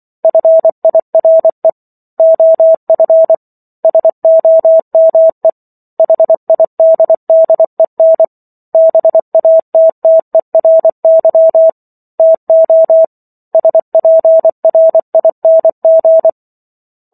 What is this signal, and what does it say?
Morse code, heard as a keyed tone, about 24 wpm, 650 Hz. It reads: FIRE OF SOME HIDDEN BATTERY TO SPRING
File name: Book One / 15-Chapter 15 - 24 WPM